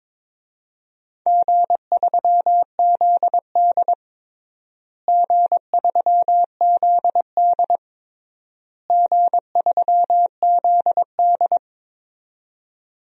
Morse code, heard as a keyed tone, about 21 wpm, 700 Hz. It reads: G3ZD G3ZD G3ZD
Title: DXCC-100 Call Signs - 4 Characters - Repeated 3x - 22wpm